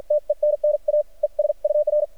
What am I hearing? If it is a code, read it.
Morse code, heard as a keyed tone, about 53 wpm, spaced as for 45 wpm, 600 Hz. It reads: TENNA ES UR
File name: archived / CRSnet